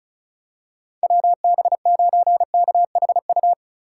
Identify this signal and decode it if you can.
Morse code, heard as a keyed tone, about 36 wpm, 700 Hz. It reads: WB9KHU
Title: USA Call Signs - 2x Character Spacing - 35wpm